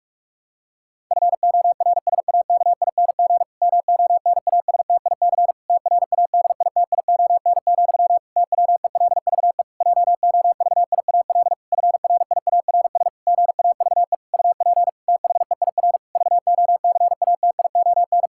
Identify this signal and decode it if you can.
Morse code, heard as a keyed tone, about 43 wpm, 700 Hz. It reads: FORSAKING MONASTIC TRADITION, TWELVE JOVIAL FRIARS GAVE UP THEIR VOCATION
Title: Pangram Challenge v2 - 45wpm